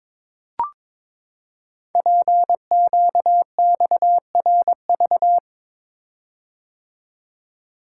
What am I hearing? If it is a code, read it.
Morse code, heard as a keyed tone, about 22 wpm, 700 Hz. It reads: PQXRV
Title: Code Groups - 5 Letters - 22wpm